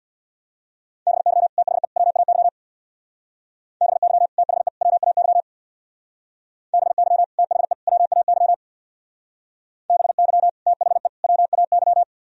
Morse code, heard as a keyed tone, about 60 wpm, 700 Hz. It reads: BY THE WAY BY THE WAY BY THE WAY BY THE WAY
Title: Speed-Racing - Sets of 3 Words 50wpm